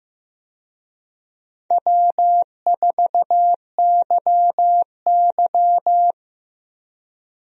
Morse code, heard as a keyed tone, about 15 wpm, 700 Hz. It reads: W4YY
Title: USA Call Signs - 4 Characters - Repeated 3x - 15wpm